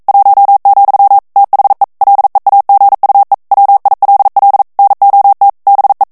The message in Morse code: 1, THE LEAGUE WILL NOT BE